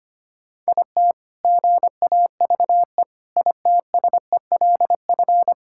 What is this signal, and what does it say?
Morse code, heard as a keyed tone, about 25 wpm, 700 Hz. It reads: IT GAVE ITSELF
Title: Sentences from Top 1000 Words - Review Part 5 - 25wpm